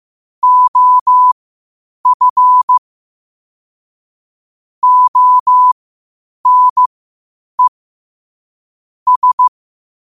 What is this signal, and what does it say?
Morse code, heard as a keyed tone, about 15 wpm, spaced as for 8 wpm, 1000 Hz. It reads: OF ONE S
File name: qotdfiles / 8wpm